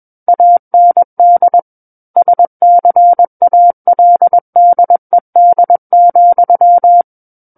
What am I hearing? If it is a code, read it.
Morse code, heard as a keyed tone, about 21 wpm, 700 Hz. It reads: AND SCALDED,
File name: Book One / 12-Chapter 12 - 21 WPM